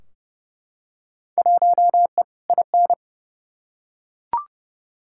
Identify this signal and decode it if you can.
Morse code, heard as a keyed tone, about 32 wpm, 700 Hz. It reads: 1E IN
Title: Field Day Exchanges - 30wpm